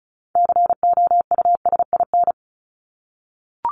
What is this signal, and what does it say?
Morse code, heard as a keyed tone, about 34 wpm, 700 Hz. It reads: COUSIN
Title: Top 1501-1750 Words 35wpm